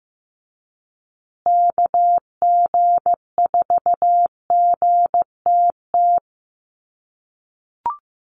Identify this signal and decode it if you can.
Morse code, heard as a keyed tone, about 14 wpm, 700 Hz. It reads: KG4GTT
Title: USA Call Signs Repeated 2x 15wpm